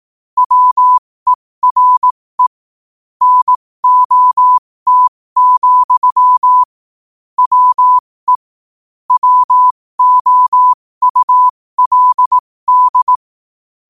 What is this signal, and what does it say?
Morse code, heard as a keyed tone, about 18 wpm, spaced as for 15 wpm, 1000 Hz. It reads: WERE NOT, WE WOULD
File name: qotdfiles / 15wpm